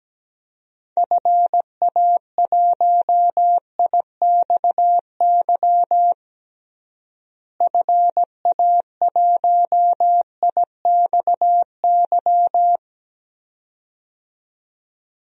Morse code, heard as a keyed tone, about 17 wpm, 700 Hz. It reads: FA1IXY FA1IXY
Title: DXCC-100 Call Signs - 6 Characters - Repeated 2x - 17wpm